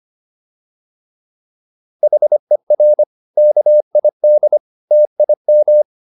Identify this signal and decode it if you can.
Morse code, heard as a keyed tone, about 25 wpm, 600 Hz. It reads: HER KID TIM